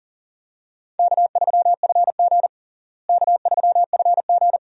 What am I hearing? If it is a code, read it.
Morse code, heard as a keyed tone, about 37 wpm, 700 Hz. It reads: K3FG K3FG
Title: USA Call Signs - 4 Characters - Repeated 2x - 40wpm